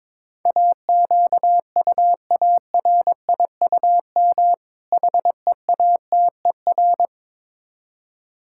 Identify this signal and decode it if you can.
Morse code, heard as a keyed tone, about 22 wpm, 700 Hz. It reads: AQUARIUM HEATER
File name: Difficult Words with Hints - 22wpm